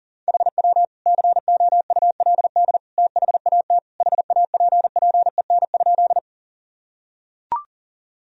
Morse code, heard as a keyed tone, about 40 wpm, 700 Hz. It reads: HW COULD THAT HAPPEN?
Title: Sentences from Top 400 Words and Abbreviations 40wpm